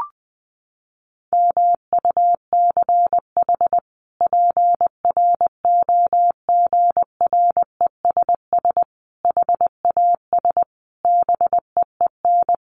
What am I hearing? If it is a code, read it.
Morse code, heard as a keyed tone, about 20 wpm, 700 Hz. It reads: MUCH PROGRESS HAS BEEN
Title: Sentences from Top 1400 Words 20wpm